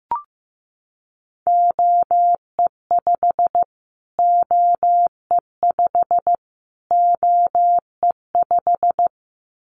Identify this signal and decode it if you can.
Morse code, heard as a keyed tone, about 15 wpm, 700 Hz. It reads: OE5 OE5 OE5
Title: Callsign Prefixes for Top 25 DXCC - Repeated 3x - 15wpm